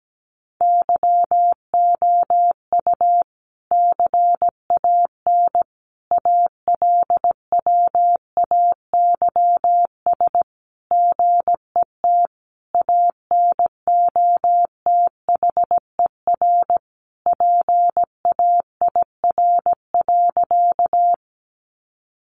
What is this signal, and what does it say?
Morse code, heard as a keyed tone, about 17 wpm, 700 Hz. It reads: YOU CAN ALWAYS GET ANOTHER PAIR.
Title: Sentences from Top 1200 Words 17wpm